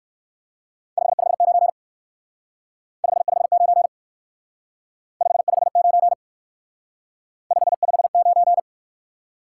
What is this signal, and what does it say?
Morse code, heard as a keyed tone, about 53 wpm, 700 Hz. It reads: 559 559 559 559
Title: Speed-Racing Top 180 QSO Elements 45wpm